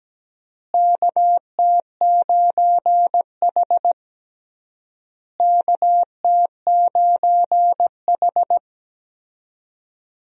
Morse code, heard as a keyed tone, about 17 wpm, 700 Hz. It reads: KT9H KT9H
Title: USA Call Signs - 4 Characters - Repeated 3x - 17wpm